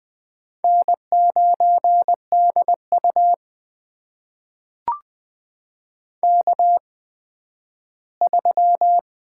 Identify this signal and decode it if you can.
Morse code, heard as a keyed tone, about 20 wpm, spaced as for 12 wpm, 700 Hz. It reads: N9DU K3
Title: USA Call Signs - 4 Characters - 8x Character Spacing - 20wpm